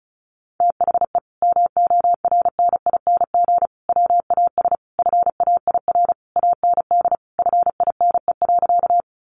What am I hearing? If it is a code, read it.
Morse code, heard as a keyed tone, about 34 wpm, 700 Hz. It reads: THE MORNING WAS FAIR AND FINE.
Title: Sentences from Top 1300 Words 35wpm